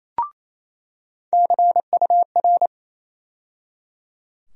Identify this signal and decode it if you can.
Morse code, heard as a keyed tone, about 28 wpm, 700 Hz. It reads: CUR
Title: USA Call Sign Postfixes - 3 Letters 28wpm